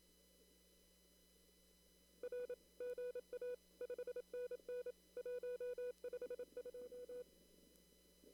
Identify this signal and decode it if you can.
Morse code, heard as a keyed tone, about 28 wpm, 490 Hz. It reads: R GA 5NN 152
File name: CW / kt5j wpx cw-op-k5pi